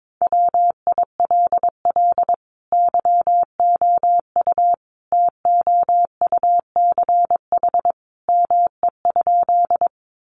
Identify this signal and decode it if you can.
Morse code, heard as a keyed tone, about 22 wpm, 700 Hz. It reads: WILL YOU TOUCH ME?
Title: Sentences from Top 1200 Words 22wpm